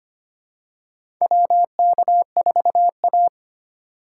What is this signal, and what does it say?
Morse code, heard as a keyed tone, about 25 wpm, 700 Hz. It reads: WK4A
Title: USA Call Signs - 4x Character Spacing - 25wpm